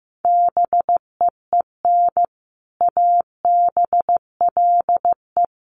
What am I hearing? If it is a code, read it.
Morse code, heard as a keyed tone, about 15 wpm, 700 Hz. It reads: BEEN ABLE
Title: Sets of 3 Words 15wpm